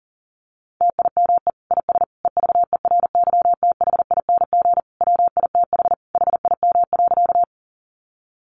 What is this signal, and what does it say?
Morse code, heard as a keyed tone, about 40 wpm, 700 Hz. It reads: TIME IS EVERYTHING WITH HIM.
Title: Sentences from Top 700 Words 40wpm